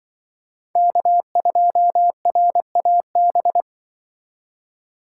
Morse code, heard as a keyed tone, about 24 wpm, 700 Hz. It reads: K2RAB
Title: Speed Racing - USA 1x3 Call Signs 20wpm